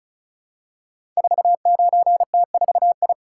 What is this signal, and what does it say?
Morse code, heard as a keyed tone, about 36 wpm, 700 Hz. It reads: 49T4I
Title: Code Groups - 5 Letter:Numbers - 35wpm